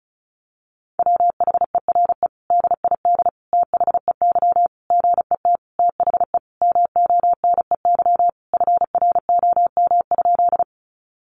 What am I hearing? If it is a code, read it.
Morse code, heard as a keyed tone, about 34 wpm, 700 Hz. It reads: WHERE DID THEY GET THE MONEY FROM?
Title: Sentences from Top 400 Words 35wpm